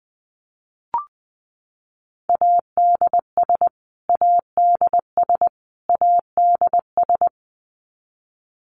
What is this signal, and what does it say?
Morse code, heard as a keyed tone, about 20 wpm, 700 Hz. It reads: ADS ADS ADS
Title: Abbreviations Repeated 3x - 20wpm